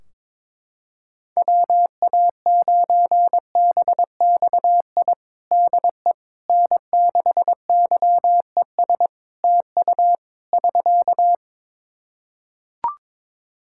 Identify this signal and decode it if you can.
Morse code, heard as a keyed tone, about 22 wpm, 700 Hz. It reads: WA9BXI DE N6YES TU <SK>
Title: Practice QSO - Exchange 4 - Ending - 22wpm